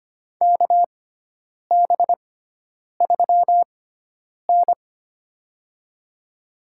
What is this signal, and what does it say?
Morse code, heard as a keyed tone, about 25 wpm, spaced as for 9 wpm, 700 Hz. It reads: KB3N
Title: USA Call Signs - 4 Characters - 6x Character Spacing - 25wpm